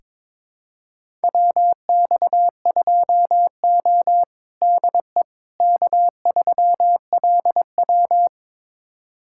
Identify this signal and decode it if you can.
Morse code, heard as a keyed tone, about 22 wpm, 700 Hz. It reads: WX2O DE K3LW
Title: USA Callsign DE Callsign - 4 Characters - 22wpm